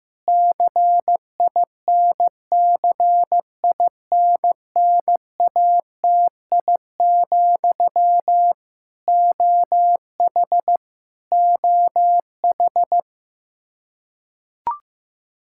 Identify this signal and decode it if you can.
Morse code, heard as a keyed tone, about 15 wpm, 700 Hz. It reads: CINCINNATI, OH OH
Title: Practice QSO - Exchange 1 - Beginning - 15wpm